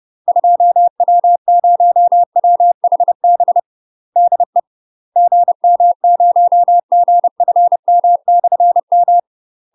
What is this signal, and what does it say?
Morse code, heard as a keyed tone, about 30 wpm, 700 Hz. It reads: 2W0WHB DE GM0GFM/M
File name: UK MASTER QSOs TRAINING / 30WPM / UK MASTER QSOs 0049